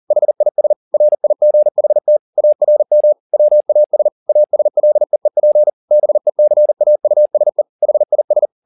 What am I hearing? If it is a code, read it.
Morse code, heard as a keyed tone, about 40 wpm, 600 Hz. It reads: HIS RIGHT ARM WAS ASLEEP BECAUSE HIS
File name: the old man and the sea 2 40wpm